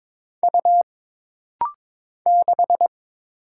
Morse code, heard as a keyed tone, about 22 wpm, 700 Hz. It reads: U 6
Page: This rapid-fire timing is geared to push your ICR (Instant Character Recognition) skills.